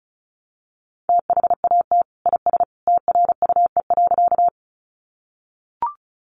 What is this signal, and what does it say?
Morse code, heard as a keyed tone, about 34 wpm, 700 Hz. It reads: THAT IS TRUE.
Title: Sentences from Top 400 Words 35wpm